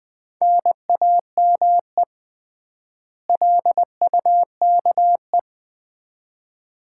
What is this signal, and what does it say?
Morse code, heard as a keyed tone, about 20 wpm, 700 Hz. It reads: NAME LUKE
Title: Common QSO Phrases - 3x Word-Spacing - 20wpm